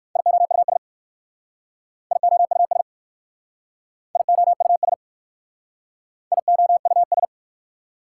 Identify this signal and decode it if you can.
Morse code, heard as a keyed tone, about 54 wpm, 700 Hz. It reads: IOUS IOUS IOUS IOUS
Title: Speed-Racing Prefixes and Suffixes 45wpm